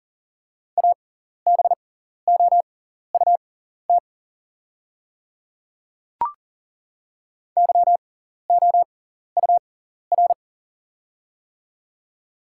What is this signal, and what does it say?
Morse code, heard as a keyed tone, about 40 wpm, spaced as for 14 wpm, 700 Hz. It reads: ABOUT YOUR
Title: 6x Character Spacing - Top 100 Words - 40wpm